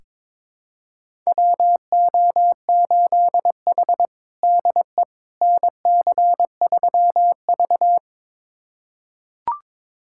Morse code, heard as a keyed tone, about 22 wpm, 700 Hz. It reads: WO8H DE NC3V
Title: USA Callsign DE Callsign - 4 Characters - 22wpm